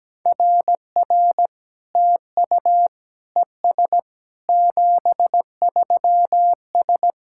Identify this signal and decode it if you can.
Morse code, heard as a keyed tone, about 17 wpm, 700 Hz. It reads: RR TU ES 73S
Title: SOTA QSO Practice 17wpm